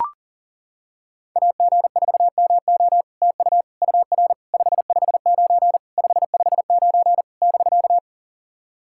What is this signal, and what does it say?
Morse code, heard as a keyed tone, about 40 wpm, 700 Hz. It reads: AG4MO TU UR 559 559 <BK>
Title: SOTA QSO Practice 40wpm